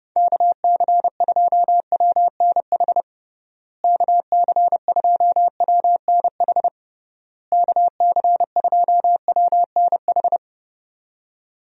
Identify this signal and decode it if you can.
Morse code, heard as a keyed tone, about 30 wpm, 700 Hz. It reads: KC2WNH KC2WNH KC2WNH
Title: USA Call Signs - 6 Characters - Repeated 3x - 30wpm